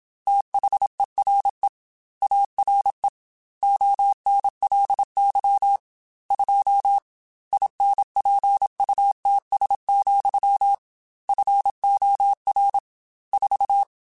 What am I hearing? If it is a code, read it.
Morse code, heard as a keyed tone, about 27 wpm, 800 Hz. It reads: THERE ARE ONLY 2 INPUTS, FOR 4
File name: Technical Subjects / Technical Subjects-mp3s / AND Gate @25